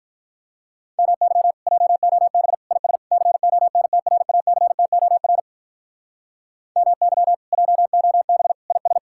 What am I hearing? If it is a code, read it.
Morse code, heard as a keyed tone, about 46 wpm, 700 Hz. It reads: MY JOB IS CONTRACTOR MY JOB IS